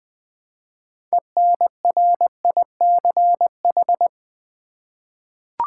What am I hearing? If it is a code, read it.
Morse code, heard as a keyed tone, about 20 wpm, 700 Hz. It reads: ENRICH
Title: 6 Letter Words - First 3 Letters Given - 20wpm